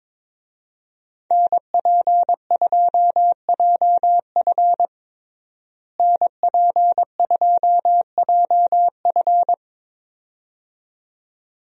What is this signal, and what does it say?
Morse code, heard as a keyed tone, about 22 wpm, 700 Hz. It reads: NP2JF NP2JF
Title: USA Call Signs - 5 Characters - Repeated 2x - 22wpm